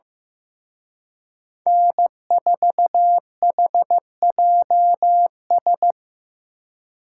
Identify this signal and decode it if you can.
Morse code, heard as a keyed tone, about 18 wpm, 700 Hz. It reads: N4HJS
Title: Speed Racing - USA 1x3 Call Signs 15wpm